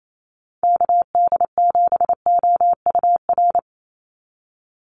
Morse code, heard as a keyed tone, about 28 wpm, 700 Hz. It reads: KD7OUR
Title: USA Call Signs 28wpm